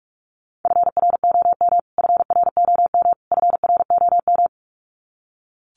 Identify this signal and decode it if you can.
Morse code, heard as a keyed tone, about 44 wpm, 700 Hz. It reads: FROM FROM FROM
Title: Top 300 Words Repeated 45wpm